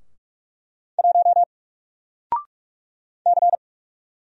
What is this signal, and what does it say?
Morse code, heard as a keyed tone, about 44 wpm, 700 Hz. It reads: 1 C
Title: Single Letter-Number 45wpm